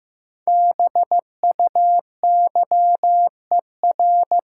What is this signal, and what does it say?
Morse code, heard as a keyed tone, about 15 wpm, 700 Hz. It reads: BUYER
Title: Five Letter Words - 15wpm